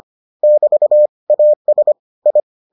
Morse code, heard as a keyed tone, about 25 wpm, 600 Hz. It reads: <BT> AS I